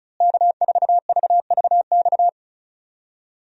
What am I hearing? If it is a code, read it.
Morse code, heard as a keyed tone, about 36 wpm, 700 Hz. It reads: K4VVX
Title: USA Call Signs - 5 Characters - Repeated 3x - 35wpm